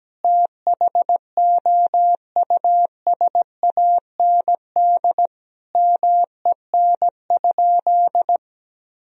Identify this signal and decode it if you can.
Morse code, heard as a keyed tone, about 17 wpm, 700 Hz. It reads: THOUSAND MEN?
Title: Sentences from Top 1000 Words - Review Part 3 - 17wpm